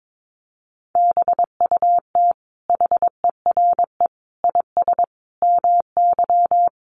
Morse code, heard as a keyed tone, about 22 wpm, 700 Hz. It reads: BUT HERE IS MY
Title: Sentences from Top 1500 Words 22wpm